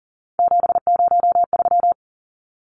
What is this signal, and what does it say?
Morse code, heard as a keyed tone, about 38 wpm, 700 Hz. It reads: 703
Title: Numbers - 3-Digit 40wpm